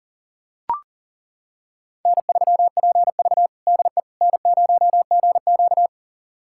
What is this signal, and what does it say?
Morse code, heard as a keyed tone, about 40 wpm, 700 Hz. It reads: N3PV DE N0GQ
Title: USA Callsign DE Callsign - 4 Characters - 40wpm